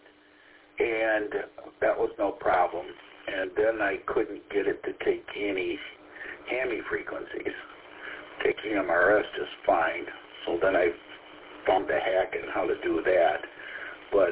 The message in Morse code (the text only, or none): none